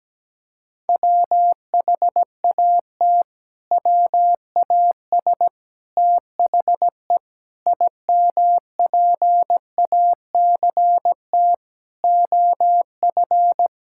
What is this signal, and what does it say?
Morse code, heard as a keyed tone, about 17 wpm, 700 Hz. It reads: WHAT WAS THE IMPACT OF
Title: Sentences from Top 1000 Words - Review Part 6 - 17wpm